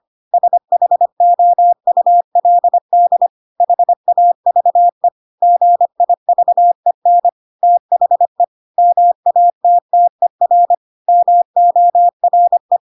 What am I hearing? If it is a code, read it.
Morse code, heard as a keyed tone, about 25 wpm, 700 Hz. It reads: SHOULD HAVE GIVEN THE MATTER MORE
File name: Pinocchio / Pinocchio ch3 25 25 ik1hge